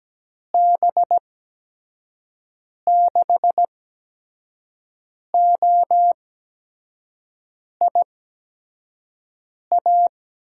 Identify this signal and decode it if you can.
Morse code, heard as a keyed tone, about 17 wpm, spaced as for 5 wpm, 700 Hz. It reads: B6OIA
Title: USA Call Signs - 8x Character Spacing - 17wpm